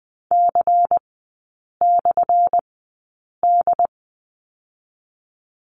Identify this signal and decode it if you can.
Morse code, heard as a keyed tone, about 20 wpm, 700 Hz. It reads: C / D
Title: 178 Lesson 29e - Slash - Three Characters 20wpm